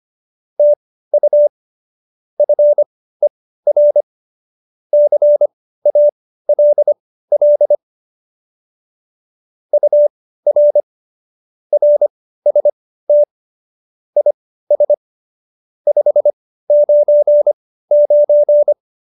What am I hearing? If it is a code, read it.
Morse code, heard as a keyed tone, about 25 wpm, spaced as for 15 wpm, 590 Hz. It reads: TU FER CALL UR RST IS 599